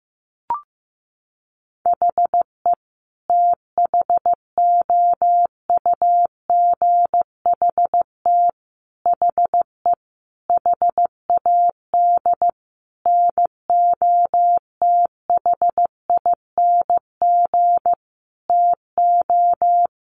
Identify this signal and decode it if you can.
Morse code, heard as a keyed tone, about 15 wpm, 700 Hz. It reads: HE THOUGHT HE HAD NOTHING TO
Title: Sentences from Top 600 Words 15wpm